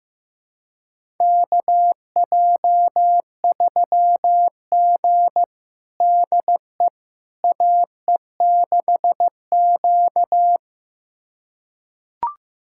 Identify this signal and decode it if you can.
Morse code, heard as a keyed tone, about 15 wpm, 700 Hz. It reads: KJ3G DE AE6Q
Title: USA Callsign DE Callsign - 4 Characters - 15wpm